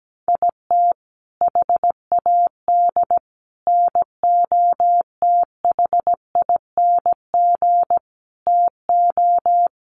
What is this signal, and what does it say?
Morse code, heard as a keyed tone, about 17 wpm, 700 Hz. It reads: IT HAD NOTHING TO
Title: Sets of 5 Words Encore2 17wpm